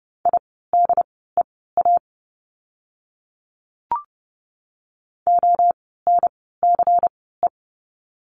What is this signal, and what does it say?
Morse code, heard as a keyed tone, about 30 wpm, spaced as for 17 wpm, 700 Hz. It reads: IDEA ONCE
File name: course / 30:10wpm / 108 Lesson 18f - Letter m - Words 30wpm